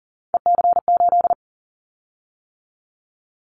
Morse code, heard as a keyed tone, about 40 wpm, 700 Hz. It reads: EC8
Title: Callsign Prefixes for Top 100 DXCC - 40wpm